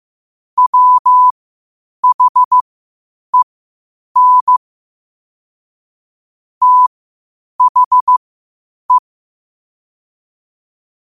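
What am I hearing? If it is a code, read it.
Morse code, heard as a keyed tone, about 15 wpm, spaced as for 8 wpm, 1000 Hz. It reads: WHEN THE
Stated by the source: Morse practice at 8 WPM from Morse Resource.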